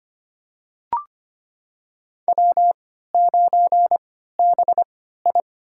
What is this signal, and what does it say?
Morse code, heard as a keyed tone, about 25 wpm, spaced as for 14 wpm, 700 Hz. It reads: W9BI
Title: USA Call Signs - 4 Characters - 3x Character Spacing - 25wpm